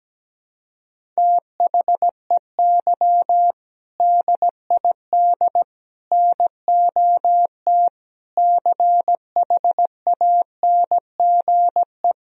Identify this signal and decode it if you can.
Morse code, heard as a keyed tone, about 17 wpm, 700 Hz. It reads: THEY DID NOT CHANGE
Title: Sentences from Top 1000 Words - Review Part 1 - 17wpm